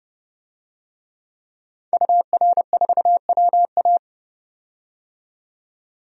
Morse code, heard as a keyed tone, about 30 wpm, 700 Hz. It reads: UR4WA
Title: DXCC-100 Call Signs - 5 Characters - 30wpm